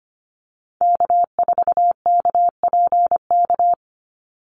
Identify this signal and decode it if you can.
Morse code, heard as a keyed tone, about 25 wpm, 700 Hz. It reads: K4KPK
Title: NA SOTA Activator Callsigns 25wpm